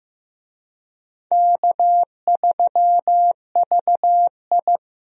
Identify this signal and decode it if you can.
Morse code, heard as a keyed tone, about 15 wpm, 700 Hz. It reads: K3VI
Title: USA Call Signs - 4 Characters - 15wpm